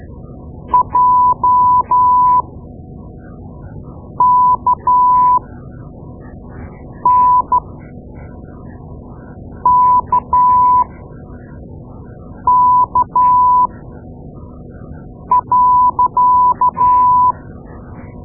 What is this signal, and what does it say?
Morse code, heard as a keyed tone, about 13 wpm, spaced as for 4 wpm, 1000 Hz. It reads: JKNKK.
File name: Morse Lessons / Morse Lessons / Tape-2A